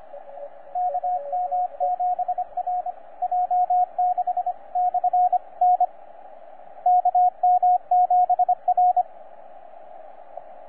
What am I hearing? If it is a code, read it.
Morse code, heard as a keyed tone, about 25 wpm, 720 Hz, with radio noise. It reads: KM7R J6/N KM7R
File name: J6/N7QT 20CW